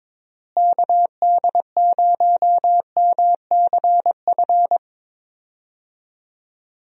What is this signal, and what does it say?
Morse code, heard as a keyed tone, about 21 wpm, 700 Hz. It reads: KD0MCF
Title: USA Call Signs - 6 Characters - Repeated 2x - 22wpm